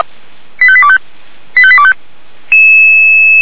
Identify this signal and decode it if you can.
Five-tone selective call (ZVEI), heard as quick runs of five tones, tones of about 80 ms. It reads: 85825 85825
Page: /464/SDR/ZVEI/